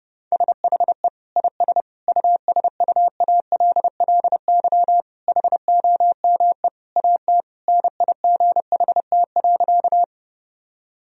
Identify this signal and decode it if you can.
Morse code, heard as a keyed tone, about 28 wpm, 700 Hz. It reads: SHE IS USUALLY HOME AT NIGHT.
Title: Sentences from Top 1000 Words - Review Part 3 - 30wpm